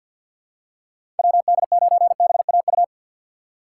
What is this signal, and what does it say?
Morse code, heard as a keyed tone, about 54 wpm, 700 Hz. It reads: WD9BAU
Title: USA Call Signs - 6 Characters - 6x Character Spacing - 50wpm